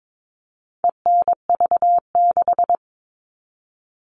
Morse code, heard as a keyed tone, about 22 wpm, 700 Hz. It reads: ENV6
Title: All Cut Numbers - 4-Digit 22wpm